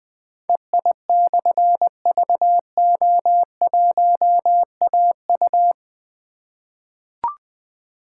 Morse code, heard as a keyed tone, about 20 wpm, 700 Hz. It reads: EI/VO1AU
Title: Difficult Call Signs 20wpm